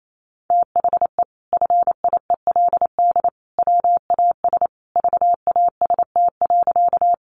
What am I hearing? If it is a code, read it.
Morse code, heard as a keyed tone, about 28 wpm, 700 Hz. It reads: THE FIELD WAS VAST.
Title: Sentences from Top 1300 Words 28wpm